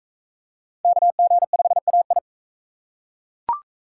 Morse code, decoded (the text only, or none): KG5AI